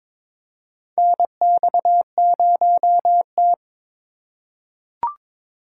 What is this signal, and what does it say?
Morse code, heard as a keyed tone, about 22 wpm, 700 Hz. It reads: NX0T
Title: USA Call Signs - 8x Character Spacing - 22wpm